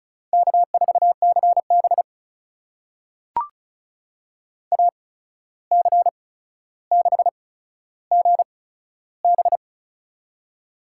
Morse code, heard as a keyed tone, about 36 wpm, spaced as for 14 wpm, 700 Hz. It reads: K4CB AC6GB
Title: USA Call Signs - 8x Character Spacing - 35wpm